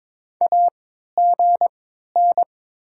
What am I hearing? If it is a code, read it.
Morse code, heard as a keyed tone, about 22 wpm, spaced as for 12 wpm, 700 Hz. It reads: AGN
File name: Top 20 QSO Elements - 3x Character Spacing - 22wpm